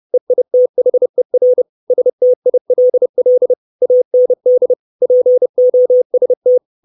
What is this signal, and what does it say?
Morse code, heard as a keyed tone, about 30 wpm, 500 Hz. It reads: EITHER STILL AND POST
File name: 632 Words 30WPM